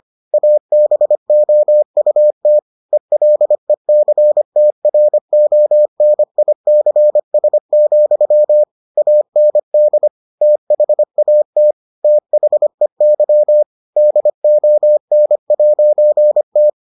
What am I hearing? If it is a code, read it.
Morse code, heard as a keyed tone, about 25 wpm, 600 Hz. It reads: ABOUT ELECTRONICS, AND THAT THEY DON'T